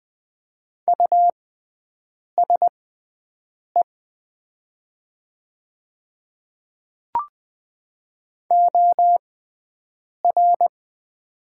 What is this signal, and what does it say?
Morse code, heard as a keyed tone, about 20 wpm, spaced as for 7 wpm, 700 Hz. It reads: USE OR
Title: 6x Character Spacing - Top 100 Words - 20wpm